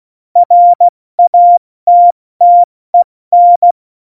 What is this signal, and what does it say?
Morse code, heard as a keyed tone, about 16 wpm, 700 Hz. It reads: RATTEN
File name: Morsekurs / Witze / Witz 08. Januar 2019, 70 F80